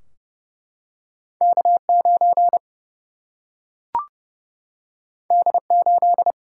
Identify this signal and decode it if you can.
Morse code, heard as a keyed tone, about 30 wpm, 700 Hz. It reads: K9 D8
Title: Single-Letter Single-Number 30wpm